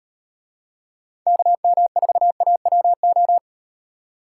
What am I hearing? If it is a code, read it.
Morse code, heard as a keyed tone, about 38 wpm, 700 Hz. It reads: KM4AWO